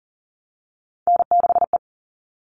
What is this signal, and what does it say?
Morse code, heard as a keyed tone, about 40 wpm, 700 Hz. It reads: N6E
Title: All Cut Numbers - 3-Digit 40wpm